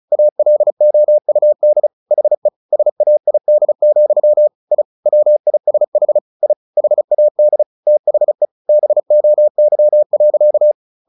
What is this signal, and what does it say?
Morse code, heard as a keyed tone, about 34 wpm, 600 Hz. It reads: ALOUD HE SAID, I WISH I HAD THE BOY.
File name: the old man and the sea / the old man and the sea 1 35wpm